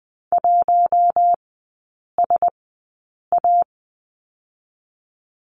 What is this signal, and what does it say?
Morse code, heard as a keyed tone, about 20 wpm, spaced as for 8 wpm, 700 Hz. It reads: 1SA
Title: 041 Lesson 8e - Number 1 - Three Characters 20wpm